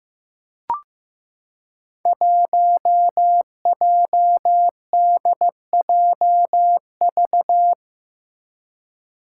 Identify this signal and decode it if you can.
Morse code, heard as a keyed tone, about 15 wpm, 700 Hz. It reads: 1JDJV
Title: Five Challenging Letter-Number Combinations 15wpm